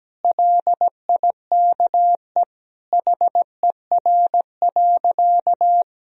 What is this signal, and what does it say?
Morse code, heard as a keyed tone, about 17 wpm, 700 Hz. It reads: LIKE HER.
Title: Sentences from Top 1000 Words - Review Part 2 - 17wpm